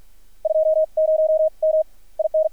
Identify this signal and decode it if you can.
Morse code, heard as a keyed tone, about 53 wpm, spaced as for 42 wpm, 630 Hz. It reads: 20M IN
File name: archived / CRSnet